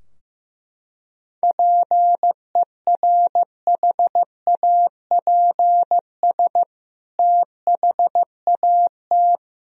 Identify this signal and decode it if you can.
Morse code, heard as a keyed tone, about 15 wpm, 700 Hz. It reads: PERHAPS THAT
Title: Sentences from Top 1000 Words - Review Part 2 - 15wpm